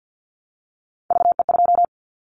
Morse code, heard as a keyed tone, about 50 wpm, 700 Hz. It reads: VE2
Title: Callsign Prefixes for Top 25 DXCC - Repeated 3x - 50wpm